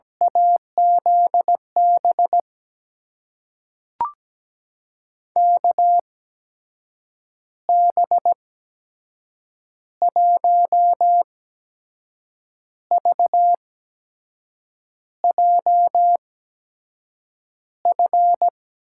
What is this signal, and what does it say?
Morse code, heard as a keyed tone, about 17 wpm, spaced as for 6 wpm, 700 Hz. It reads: AZB KB1VJF
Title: USA Call Signs - 6 Characters - 8x Character Spacing - 17wpm